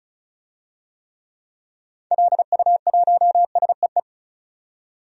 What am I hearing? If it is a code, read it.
Morse code, heard as a keyed tone, about 34 wpm, 700 Hz. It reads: LU1SEE
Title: DXCC-100 Call Signs - 6 Characters - 35wpm